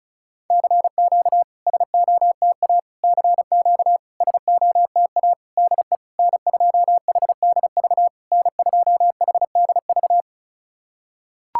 Morse code, heard as a keyed tone, about 36 wpm, 700 Hz. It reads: CQ SOTA CQ SOTA DE N2HDV N2HDV
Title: SOTA QSO Practice 35wpm